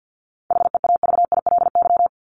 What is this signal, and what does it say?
Morse code, heard as a keyed tone, about 48 wpm, 700 Hz. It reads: HEAVILY
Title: LY Suffix - Top 100 Words with Base Words 50wpm